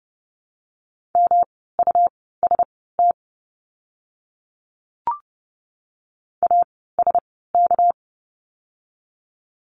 Morse code, heard as a keyed tone, about 30 wpm, spaced as for 17 wpm, 700 Hz. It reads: MUST ASK
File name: course / 30:10wpm / 237 Lesson 38f - Letter q - Words 30wpm